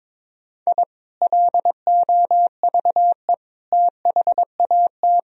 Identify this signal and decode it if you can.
Morse code, heard as a keyed tone, about 21 wpm, 700 Hz. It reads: I LOVE THAT
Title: Sentences from Top 1000 Words - Review Part 4 - 22wpm